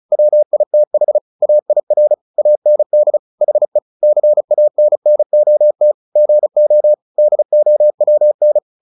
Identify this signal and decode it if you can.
Morse code, heard as a keyed tone, about 35 wpm, 600 Hz. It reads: WITH AIR AND HE CANNOT GO DOWN
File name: the old man and the sea 2 35wpm